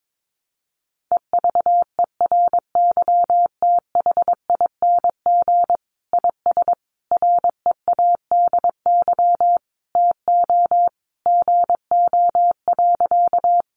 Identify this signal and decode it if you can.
Morse code, heard as a keyed tone, about 22 wpm, 700 Hz. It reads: EVERYTHING IS READY TO GO.
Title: Sentences from Top 700 Words 22wpm